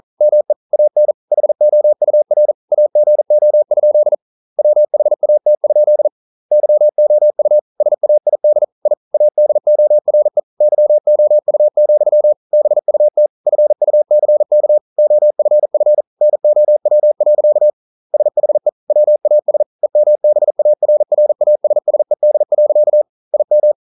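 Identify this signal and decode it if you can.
Morse code, heard as a keyed tone, about 42 wpm, 600 Hz. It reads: ME AN HOUR AGO? WHAT? YOU SAID I ADORE YOU, BUT FUCK OFF NOW. SHE WAS EMBARRASSED. IM
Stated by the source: Morse code practice at 41 wpm, from CW med Gnister.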